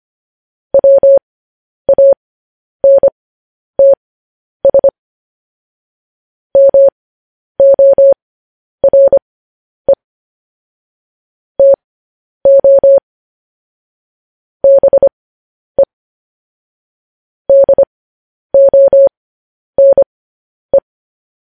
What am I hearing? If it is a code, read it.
Morse code, heard as a keyed tone, about 25 wpm, spaced as for 10 wpm, 550 Hz. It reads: WANTS MORE TO BE DONE
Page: news-headlines-in-morse-code-at-10-wpm